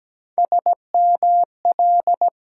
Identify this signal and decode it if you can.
Morse code, heard as a keyed tone, about 17 wpm, 700 Hz. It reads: SML
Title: USA Call Sign Postfixes - 3 Letters 17wpm